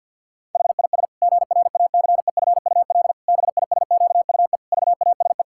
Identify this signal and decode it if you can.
Morse code, heard as a keyed tone, about 51 wpm, 700 Hz. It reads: HIS GRACEFUL BISQUE VASE
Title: Pangram Challenge v2 - 50wpm